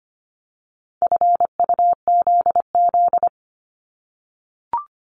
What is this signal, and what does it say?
Morse code, heard as a keyed tone, about 25 wpm, 700 Hz. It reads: FUZZ
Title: Multi-letter Phonemes 25wpm